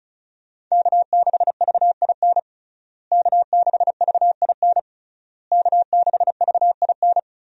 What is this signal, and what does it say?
Morse code, heard as a keyed tone, about 36 wpm, 700 Hz. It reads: K6VIN K6VIN K6VIN
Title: USA Call Signs - 5 Characters - Repeated 3x - 35wpm